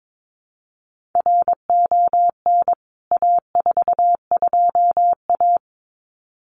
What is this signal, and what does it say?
Morse code, heard as a keyed tone, about 22 wpm, 700 Hz. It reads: RON A42A
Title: CWT Exchange 22wpm